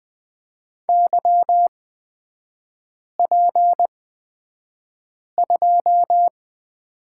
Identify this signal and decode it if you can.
Morse code, heard as a keyed tone, about 20 wpm, spaced as for 5 wpm, 690 Hz. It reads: YP2
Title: 236 Lesson 38e - Letter q - Three Characters 20wpm